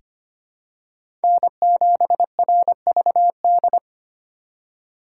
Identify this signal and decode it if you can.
Morse code, heard as a keyed tone, about 25 wpm, 700 Hz. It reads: N7RVD
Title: K1USN SST QSO Practice - 25wpm